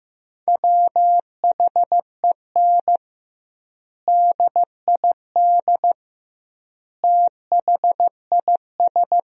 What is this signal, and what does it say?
Morse code, heard as a keyed tone, about 16 wpm, 700 Hz. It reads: WHEN DID THIS
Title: 2x Word-Spacing - Sentences from Top 400 Words - 15wpm